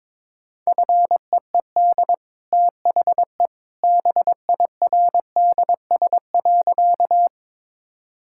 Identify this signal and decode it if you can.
Morse code, heard as a keyed tone, about 22 wpm, 700 Hz. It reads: FEED THE BIRDS.
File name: practice / 22wpm / Easy Sentences with Hints - 22wpm